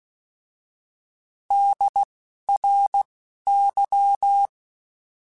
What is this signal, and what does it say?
Morse code, heard as a keyed tone, about 16 wpm, 800 Hz. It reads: D R Y
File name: Alphabetical / Alphabetical mp3s / Ds @10